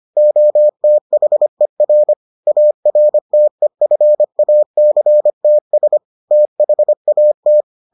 Morse code, heard as a keyed tone, about 25 wpm, 600 Hz. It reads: OTHER ARTEFACTS THAT